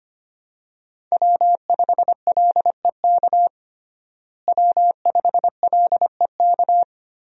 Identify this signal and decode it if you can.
Morse code, heard as a keyed tone, about 25 wpm, 700 Hz. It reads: W5LEK W5LEK
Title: USA Call Signs - 5 Characters - Repeated 3x - 25wpm